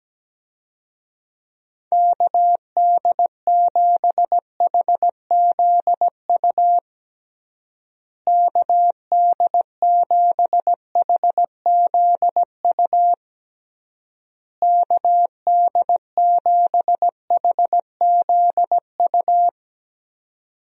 Morse code, heard as a keyed tone, about 17 wpm, 700 Hz. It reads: KD7HZU KD7HZU KD7HZU
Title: USA Call Signs - 6 Characters - Repeated 3x - 17wpm